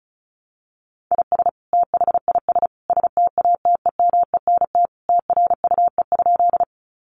Morse code, heard as a keyed tone, about 34 wpm, 700 Hz. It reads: IS THIS STATEMENT TRUE?
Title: Sentences from Top 1200 Words 35wpm